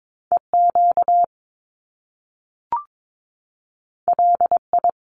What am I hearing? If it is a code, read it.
Morse code, heard as a keyed tone, about 22 wpm, 700 Hz. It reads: EQ LI
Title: Top Two Letter Combinations in Words 22wpm